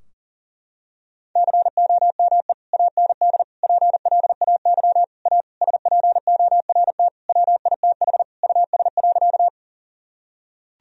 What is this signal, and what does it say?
Morse code, heard as a keyed tone, about 37 wpm, 700 Hz. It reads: COME AND PLAY A SPORT WITH US.
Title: Sentences from Top 1000 Words - Review Part 4 - 40wpm